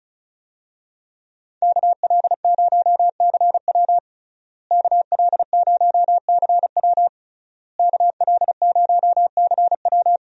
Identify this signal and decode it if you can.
Morse code, heard as a keyed tone, about 36 wpm, 700 Hz. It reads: KL0CW KL0CW KL0CW
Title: USA Call Signs - 5 Characters - Repeated 3x - 35wpm